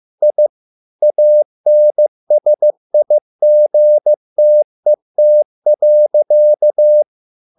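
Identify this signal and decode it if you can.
Morse code, heard as a keyed tone, about 15 wpm, 600 Hz. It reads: I ANSIGTET.
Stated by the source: Morse code at a slow 15 wpm from CW med Gnister.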